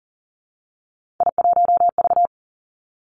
Morse code, heard as a keyed tone, about 40 wpm, 700 Hz. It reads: I1V
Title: Three Challenging Letter-Number Combinations 40wpm